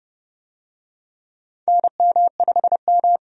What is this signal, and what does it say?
Morse code, heard as a keyed tone, about 32 wpm, 700 Hz. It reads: NM5M
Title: USA Call Signs - 4 Characters - 30wpm